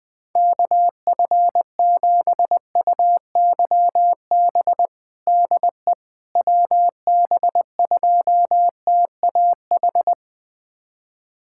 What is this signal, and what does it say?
Morse code, heard as a keyed tone, about 21 wpm, 700 Hz. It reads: KF7UYB DE WB2TAH
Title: USA Callsign DE Callsign - 6 Characters - 20wpm